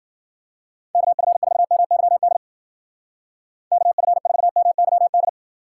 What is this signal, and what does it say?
Morse code, heard as a keyed tone, about 52 wpm, 700 Hz. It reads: KF4MYD KF4MYD